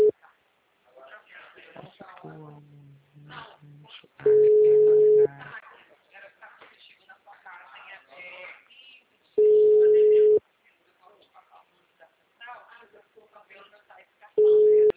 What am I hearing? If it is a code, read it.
Ringback tone (Europe, 425 Hz), with speech.
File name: out- -2033- - - .25319